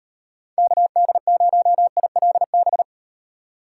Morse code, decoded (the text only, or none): KD0ILB